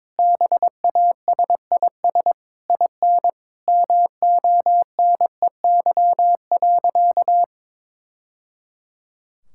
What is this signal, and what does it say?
Morse code, heard as a keyed tone, about 22 wpm, 700 Hz. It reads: BASIS IN MONEY.
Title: Sentences from Top 1000 Words - Review Part 5 - 22wpm